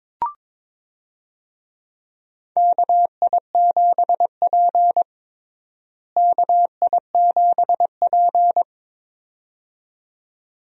Morse code, heard as a keyed tone, about 22 wpm, 700 Hz. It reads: KI7P KI7P
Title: USA Call Signs - 4 Characters - Repeated 2x - 22wpm